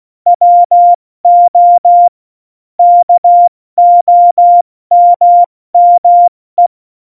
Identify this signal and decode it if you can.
Morse code, heard as a keyed tone, about 16 wpm, 700 Hz. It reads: WO KOMME
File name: Morsekurs / Witze / Witz 19. Februar 2019, 70 F80